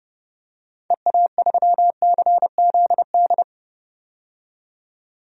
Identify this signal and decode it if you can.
Morse code, heard as a keyed tone, about 30 wpm, 700 Hz. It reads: EA3CZD
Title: DXCC-100 Call Signs - 6 Characters - 30wpm